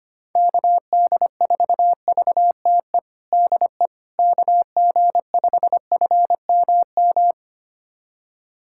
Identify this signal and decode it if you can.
Morse code, heard as a keyed tone, about 25 wpm, 700 Hz. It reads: KD4VTE DE KG5FMM
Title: USA Callsign DE Callsign - 6 Characters - 25wpm